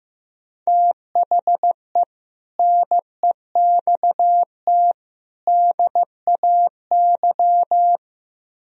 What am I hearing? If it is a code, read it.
Morse code, heard as a keyed tone, about 16 wpm, 700 Hz. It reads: THE NEXT DAY
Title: Speed-Racing - Sets of 3 Words 15wpm